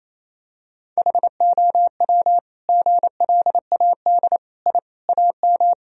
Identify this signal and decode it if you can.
Morse code, heard as a keyed tone, about 28 wpm, 700 Hz. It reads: HOW GLAD I AM
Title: Sentences from Top 1000 Words - Review Part 2 - 28wpm